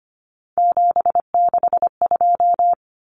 Morse code, heard as a keyed tone, about 25 wpm, 700 Hz. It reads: 762
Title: Common Cut Numbers - 3-Digit 25wpm